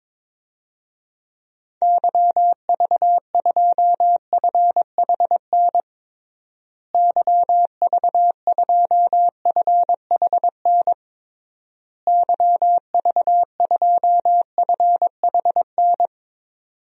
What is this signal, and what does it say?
Morse code, heard as a keyed tone, about 22 wpm, 700 Hz. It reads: YV2FHN YV2FHN YV2FHN
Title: DXCC-100 Call Signs - 6 Characters - Repeated 3x - 22wpm